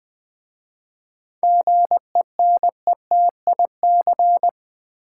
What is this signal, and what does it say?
Morse code, heard as a keyed tone, about 20 wpm, 700 Hz. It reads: GENETIC
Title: 7 Letter Words - First 3 Letters Given - 20wpm